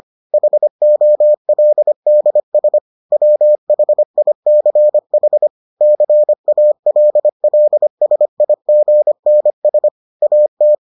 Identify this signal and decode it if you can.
Morse code, heard as a keyed tone, about 25 wpm, 600 Hz. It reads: HOLDS WHICH CALLSIGNS AT